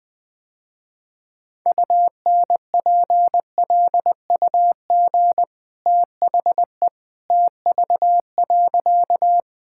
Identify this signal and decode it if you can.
Morse code, heard as a keyed tone, about 19 wpm, 700 Hz. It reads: UNPLUG THE TV.
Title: Easy Sentences with Hints - 20wpm